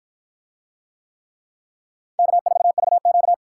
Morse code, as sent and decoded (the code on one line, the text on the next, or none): -.- ....- ...- --.-
K4VQ